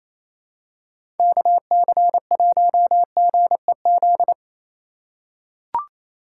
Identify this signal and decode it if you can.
Morse code, heard as a keyed tone, about 28 wpm, 700 Hz. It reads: KC1GEZ
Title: USA Call Signs - 4x Character Spacing - 28wpm